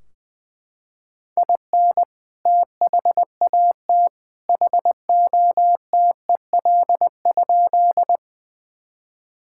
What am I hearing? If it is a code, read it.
Morse code, heard as a keyed tone, about 20 wpm, 700 Hz. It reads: IN THAT HOTEL?
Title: Sentences from Top 1600 Words 20wpm